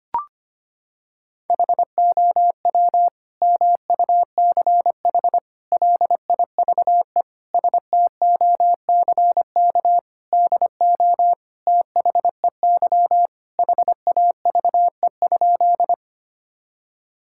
Morse code, heard as a keyed tone, about 25 wpm, 700 Hz. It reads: HOW MUCH LIVE STOCK DO THEY HAVE?
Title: Sentences from Top 1000 Words - Review Part 5 - 25wpm